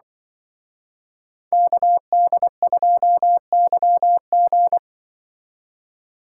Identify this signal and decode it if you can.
Morse code, heard as a keyed tone, about 25 wpm, 700 Hz. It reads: KD2YG
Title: Speed Racing - USA 2x2 Call Signs 20wpm